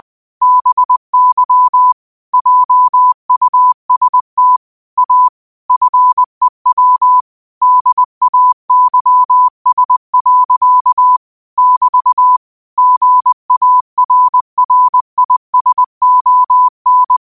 Morse code, heard as a keyed tone, about 20 wpm, 1000 Hz. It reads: BY JUST A FEW DAYS. <BT> GARRISON